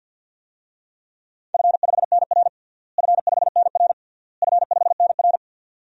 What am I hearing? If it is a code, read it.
Morse code, heard as a keyed tone, about 47 wpm, 700 Hz. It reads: F5NR F5NR F5NR
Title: DXCC-100 Call Signs - 4 Characters - Repeated 3x - 50wpm